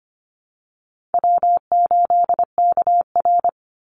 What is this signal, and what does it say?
Morse code, heard as a keyed tone, about 25 wpm, 700 Hz. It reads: W8KR
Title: Call Signs - 25wpm